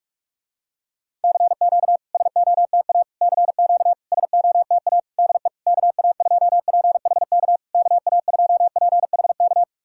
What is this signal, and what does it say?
Morse code, heard as a keyed tone, about 46 wpm, 690 Hz. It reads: CQ SOTA CQ SOTA DE KA2PHK KA2PHK
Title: SOTA QSO Practice 45wpm